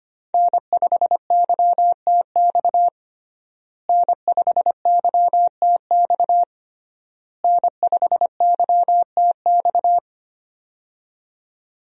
Morse code, heard as a keyed tone, about 25 wpm, 700 Hz. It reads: N5YTX N5YTX N5YTX
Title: USA Call Signs - 5 Characters - Repeated 3x - 25wpm